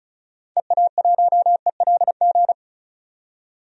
Morse code, heard as a keyed tone, about 34 wpm, 700 Hz. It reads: EA1ELG
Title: DXCC-100 Call Signs - 6 Characters - 35wpm